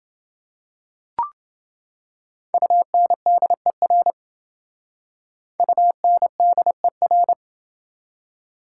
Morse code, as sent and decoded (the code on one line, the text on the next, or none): ..- -. -.. .-. / ..- -. -.. .-.
UNDER UNDER